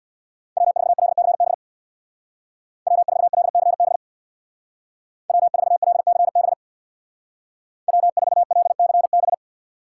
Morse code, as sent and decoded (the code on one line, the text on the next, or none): .-- ....- .-.. -.-. -... / .-- ....- .-.. -.-. -... / .-- ....- .-.. -.-. -... / .-- ....- .-.. -.-. -...
W4LCB W4LCB W4LCB W4LCB